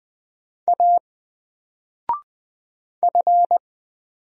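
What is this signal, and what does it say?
Morse code, heard as a keyed tone, about 20 wpm, 700 Hz. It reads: A F
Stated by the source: Single letter and number Morse drill at 20 wpm, from N4LCD.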